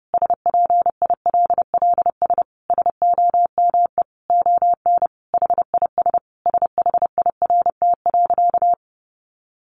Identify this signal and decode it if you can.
Morse code, heard as a keyed tone, about 30 wpm, 700 Hz. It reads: SPILLS SOME ON HIS SHIRT.
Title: Additional Easy Phrases 30wpm